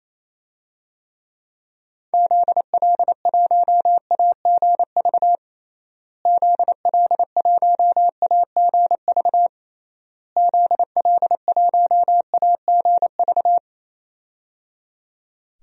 Morse code, heard as a keyed tone, about 28 wpm, 700 Hz. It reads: ZL1AGV ZL1AGV ZL1AGV
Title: DXCC-100 Call Signs - 6 Characters - Repeated 3x - 28wpm